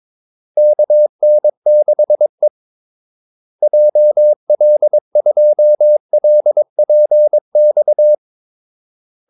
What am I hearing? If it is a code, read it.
Morse code, heard as a keyed tone, about 22 wpm, 600 Hz. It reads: KN6E JL2LPX
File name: Callsigns-22wpm